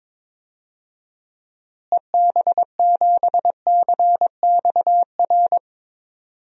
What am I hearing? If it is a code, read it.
Morse code, heard as a keyed tone, about 22 wpm, 700 Hz. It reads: EB7CXR
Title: DXCC-100 Call Signs - 6 Characters - Repeated 3x - 22wpm